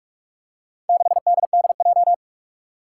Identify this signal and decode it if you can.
Morse code, heard as a keyed tone, about 44 wpm, 700 Hz. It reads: 6DDJ
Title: Four Challenging Letter-Number Combinations 45wpm